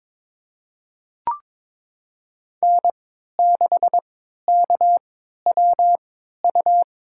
Morse code, heard as a keyed tone, about 22 wpm, spaced as for 13 wpm, 700 Hz. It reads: N6KWU
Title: USA Call Signs - 5 Characters - 3x Character Spacing - 22wpm